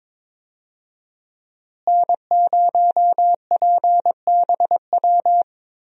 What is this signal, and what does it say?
Morse code, heard as a keyed tone, about 22 wpm, 700 Hz. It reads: N0PBW
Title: USA Call Signs - 5 Characters - Repeated 2x - 22wpm